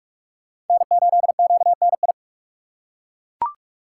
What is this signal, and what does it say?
Morse code, heard as a keyed tone, about 39 wpm, 700 Hz. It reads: N8QNI